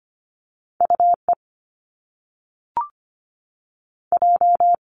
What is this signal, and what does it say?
Morse code, heard as a keyed tone, about 25 wpm, 700 Hz. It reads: UE J
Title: Top Two Letter Combinations in Words 25wpm